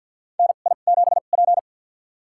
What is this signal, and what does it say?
Morse code, heard as a keyed tone, about 52 wpm, spaced as for 37 wpm, 700 Hz. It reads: NI7P
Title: USA Call Signs - 4 Characters - 2x Character Spacing - 50wpm